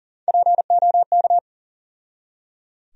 Morse code, heard as a keyed tone, about 37 wpm, 700 Hz. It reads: POK
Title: USA Call Sign Postfixes - 3 Letters 40wpm